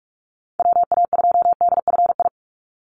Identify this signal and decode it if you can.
Morse code, heard as a keyed tone, about 44 wpm, 700 Hz. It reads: WA2DFI
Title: Call Signs Repeated 2x - 45wpm